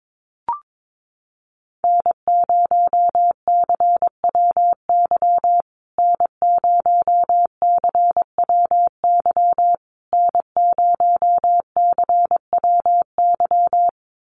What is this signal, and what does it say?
Morse code, heard as a keyed tone, about 22 wpm, 700 Hz. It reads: N0CWY N0CWY N0CWY
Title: USA Call Signs Repeated 3x 22wpm